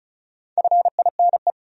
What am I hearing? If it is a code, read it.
Morse code, heard as a keyed tone, about 46 wpm, 700 Hz. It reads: FINE